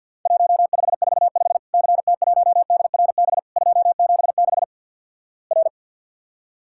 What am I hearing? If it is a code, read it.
Morse code, heard as a keyed tone, about 50 wpm, 700 Hz. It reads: 1545 CT1DRB 276 R
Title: Worked All Europe DX Contest - QTC Exchange with Repeats - 50wpm